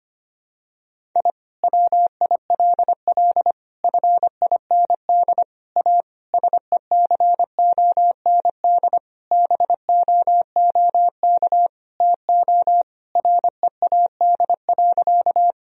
Morse code, heard as a keyed tone, about 25 wpm, 700 Hz. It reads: I WILL FIND A SECOND BOOK TO READ.
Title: Sentences from Top 1000 Words - Review Part 1 - 25wpm